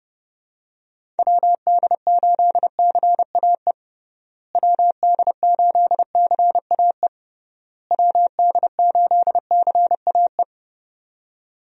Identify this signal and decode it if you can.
Morse code, heard as a keyed tone, about 30 wpm, 700 Hz. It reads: WD8CAE WD8CAE WD8CAE
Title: USA Call Signs - 6 Characters - Repeated 3x - 30wpm